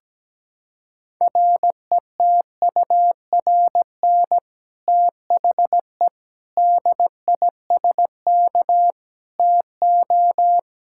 Morse code, heard as a keyed tone, about 17 wpm, 700 Hz. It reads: RETURN THE DISK TO
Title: Sentences from Top 1000 Words - Review Part 6 - 17wpm